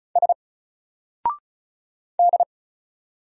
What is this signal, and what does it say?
Morse code, heard as a keyed tone, about 36 wpm, 700 Hz. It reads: S D
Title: Single Character (Letter:Number:Punctuation:Prosign) 35wpm